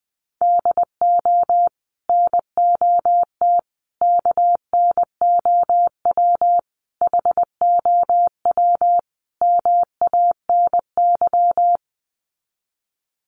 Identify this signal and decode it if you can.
Morse code, heard as a keyed tone, about 20 wpm, 700 Hz. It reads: DO NOT KNOW HOW MANY
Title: Sets of 5 Words Encore 20wpm